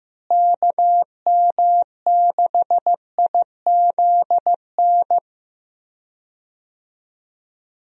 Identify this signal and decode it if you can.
Morse code, heard as a keyed tone, about 15 wpm, 700 Hz. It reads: KM6IZN
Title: USA Call Signs - 6 Characters - Repeated 3x - 15wpm